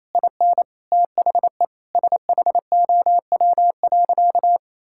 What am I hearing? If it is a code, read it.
Morse code, heard as a keyed tone, about 28 wpm, 700 Hz. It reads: IN THE SHOW.
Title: Sentences from Top 1000 Words - Review Part 6 - 28wpm